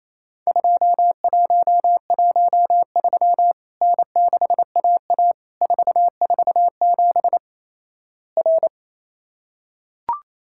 Morse code, heard as a keyed tone, about 28 wpm, 700 Hz. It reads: 2113 N6AA 447 R
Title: Worked All Europe DX Contest - QTC Exchange - 28wpm